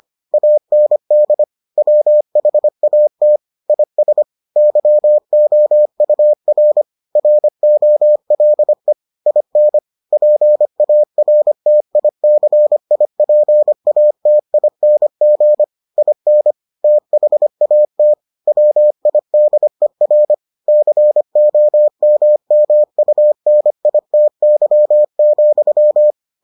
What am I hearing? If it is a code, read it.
Morse code, heard as a keyed tone, about 25 wpm, 600 Hz. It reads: AND WHAT IS YOUR ROLE IN PARTICIPATING IN THAT WIDER COMMUNITY,